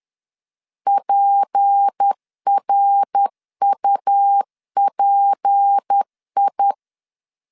Morse code, heard as a keyed tone, about 10 wpm, 800 Hz. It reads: PRUPI
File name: LEZIONI / 6 PKX0 / 6 P 10wpm 2432